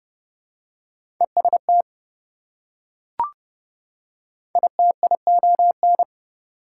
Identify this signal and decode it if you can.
Morse code, heard as a keyed tone, about 30 wpm, 700 Hz. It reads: EST ITION
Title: Prefixes and Suffixes 30wpm